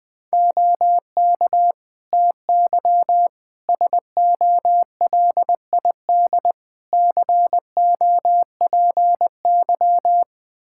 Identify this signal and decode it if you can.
Morse code, heard as a keyed tone, about 20 wpm, 700 Hz. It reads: OK TY SOLID COPY
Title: K1USN SST QSO Practice 20wpm